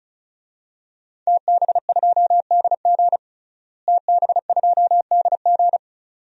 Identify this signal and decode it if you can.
Morse code, heard as a keyed tone, about 34 wpm, 700 Hz. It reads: TB2DG TB2DG
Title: DXCC-100 Call Signs - 5 Characters - Repeated 2x - 35wpm